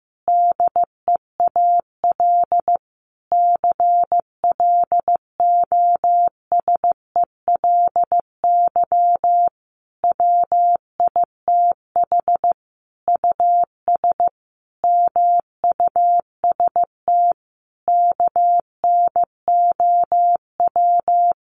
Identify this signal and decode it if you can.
Morse code, heard as a keyed tone, about 15 wpm, 700 Hz. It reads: DEAL CLOSELY WITH US MUST KNOW
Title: Sentences from Top 1400 Words 15wpm